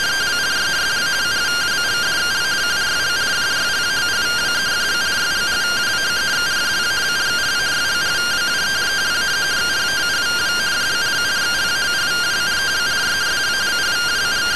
Two-tone teleprinter signal, about 170 Hz shift.